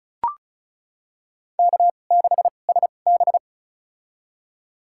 Morse code, heard as a keyed tone, about 36 wpm, spaced as for 25 wpm, 700 Hz. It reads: K6SB
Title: USA Call Signs - 4 Characters - 2x Character Spacing - 35wpm